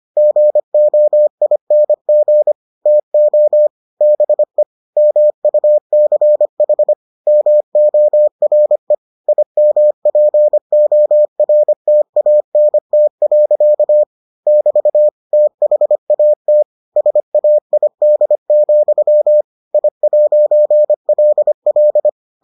Morse code, heard as a keyed tone, about 25 wpm, 600 Hz. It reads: GOING TO BE MUCH MORE IMPORTANT. <BT> THAT SAID, I'LL